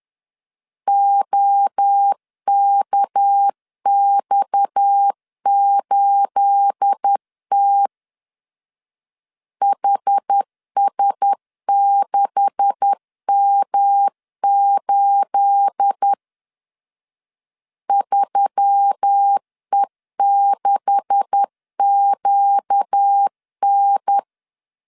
Morse code, heard as a keyed tone, about 10 wpm, 800 Hz. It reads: OKX8T HS6M8 3E6QN